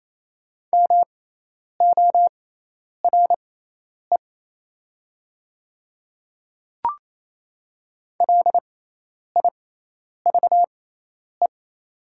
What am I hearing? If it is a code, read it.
Morse code, heard as a keyed tone, about 28 wpm, spaced as for 10 wpm, 700 Hz. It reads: MORE LIVE
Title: 6x Character Spacing - Top 300 Words - 28wpm